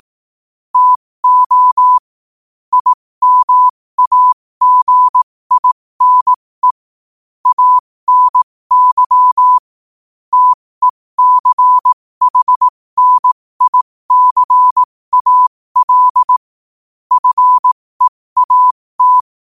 Morse code, heard as a keyed tone, about 18 wpm, spaced as for 15 wpm, 1000 Hz. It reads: TO IMAGINE ANY TECHNICAL FEAT
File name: qotdfiles / 15wpm